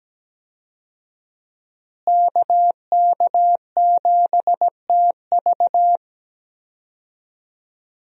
Morse code, heard as a keyed tone, about 17 wpm, 700 Hz. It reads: KK7TV
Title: USA Call Signs - 5 Characters - Repeated 3x - 17wpm